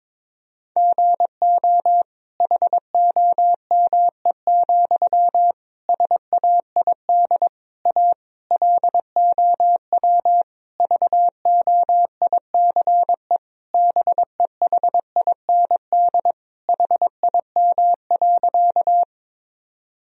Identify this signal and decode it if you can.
Morse code, heard as a keyed tone, about 22 wpm, 700 Hz. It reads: GO HOME, SAID A LOW VOICE BEHIND HIM.
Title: Sentences from Top 1000 Words - Review Part 1 - 22wpm